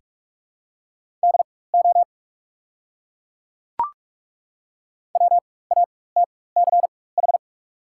Morse code, heard as a keyed tone, about 42 wpm, spaced as for 21 wpm, 690 Hz. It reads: DO WATCH
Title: 4x Character Spacing - Top 300 Words - 45wpm